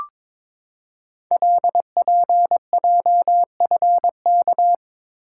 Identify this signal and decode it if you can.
Morse code, heard as a keyed tone, about 22 wpm, 700 Hz. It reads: LPJFK
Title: Code Groups - 5 Letters - 22wpm